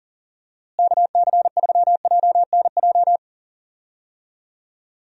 Morse code, heard as a keyed tone, about 40 wpm, 700 Hz. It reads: KC3JNJ
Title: USA Call Signs - 6 Characters - Repeated 2x - 40wpm